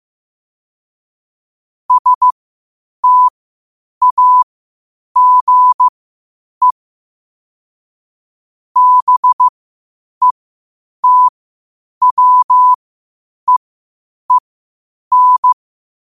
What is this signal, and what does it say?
Morse code, heard as a keyed tone, about 15 wpm, spaced as for 8 wpm, 1000 Hz. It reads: STAGE BETWEEN
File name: qotdfiles / 8wpm